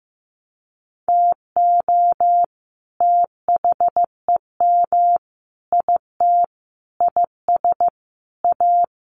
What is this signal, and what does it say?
Morse code, heard as a keyed tone, about 15 wpm, 700 Hz. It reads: TO THEM IT IS A
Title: Sentences from Top 1400 Words 15wpm